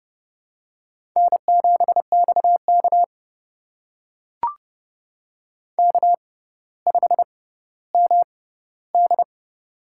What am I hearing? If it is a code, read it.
Morse code, heard as a keyed tone, about 30 wpm, spaced as for 15 wpm, 700 Hz. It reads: N7XK K5MD
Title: USA Call Signs - 4 Characters - 6x Character Spacing - 30wpm